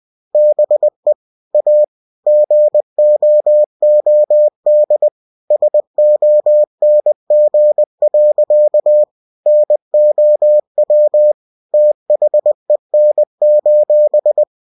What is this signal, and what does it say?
Morse code, heard as a keyed tone, about 20 wpm, 600 Hz. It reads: BE A GOOD SONG. NOW THEN: